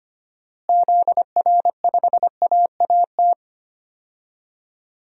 Morse code, heard as a keyed tone, about 25 wpm, 700 Hz. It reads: ZR5AAT
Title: DXCC-100 Call Signs - 6 Characters - Repeated 2x - 25wpm